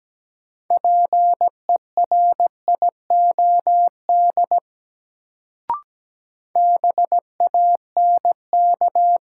Morse code, heard as a keyed tone, about 17 wpm, 700 Hz. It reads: PERIOD BANK
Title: Sets of 5 Words - New Word Drill - 17wpm